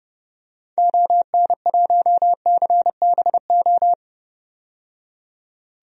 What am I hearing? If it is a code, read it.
Morse code, heard as a keyed tone, about 30 wpm, 700 Hz. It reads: ON1CBO
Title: DXCC-100 Call Signs - 6 Characters - 30wpm